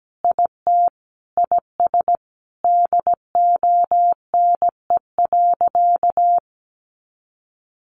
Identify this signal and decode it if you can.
Morse code, heard as a keyed tone, about 17 wpm, 700 Hz. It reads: IT IS DONE.
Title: Sentences from Top 1500 Words 17wpm